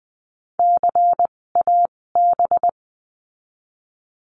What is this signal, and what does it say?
Morse code, heard as a keyed tone, about 20 wpm, spaced as for 16 wpm, 700 Hz. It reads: CAB
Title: 256 Lesson 42a - Words 12-20wpm